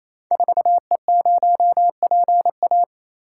Morse code, heard as a keyed tone, about 28 wpm, 700 Hz. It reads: 4E0PA
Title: Code Groups - 5 Letter:Numbers - 28wpm